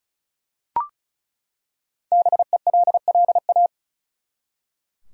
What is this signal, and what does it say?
Morse code, heard as a keyed tone, about 36 wpm, 700 Hz. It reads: BELLA
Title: USA Common Names - 35wpm